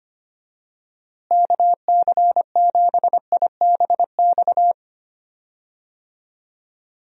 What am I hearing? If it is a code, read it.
Morse code, heard as a keyed tone, about 25 wpm, 700 Hz. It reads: KC7IBX
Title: USA Call Signs - 6 Characters - Repeated 3x - 25wpm